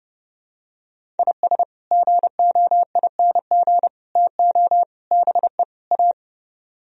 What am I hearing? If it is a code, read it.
Morse code, heard as a keyed tone, about 31 wpm, 700 Hz. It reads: IS GOING TO BE A